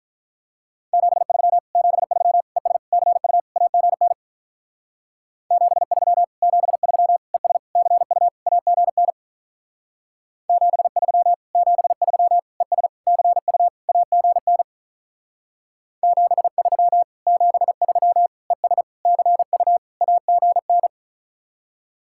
Morse code, heard as a keyed tone, about 46 wpm, 700 Hz. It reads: 73 73 ES CU AGN 73 73 ES CU AGN 73 73 ES CU AGN 73 73 ES CU AGN